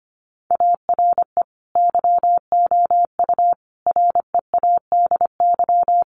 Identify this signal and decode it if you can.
Morse code, heard as a keyed tone, about 25 wpm, 700 Hz. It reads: ARE YOU READY
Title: Sentences from Top 400 Words 25wpm